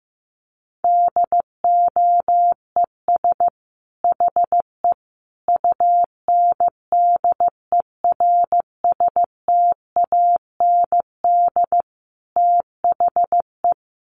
Morse code, heard as a keyed tone, about 15 wpm, 700 Hz. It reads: DOES HE UNDERSTAND THE
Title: Sentences from Top 1100 Words 15wpm